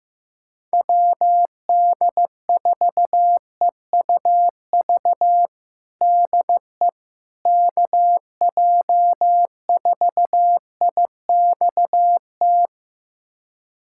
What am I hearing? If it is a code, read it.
Morse code, heard as a keyed tone, about 15 wpm, 700 Hz. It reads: WD4EUV DE KJ4IXT
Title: USA Callsign DE Callsign - 6 Characters - 15wpm